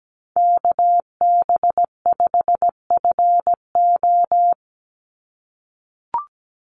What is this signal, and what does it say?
Morse code, heard as a keyed tone, about 17 wpm, 700 Hz. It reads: KB5FO
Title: USA Call Signs 17wpm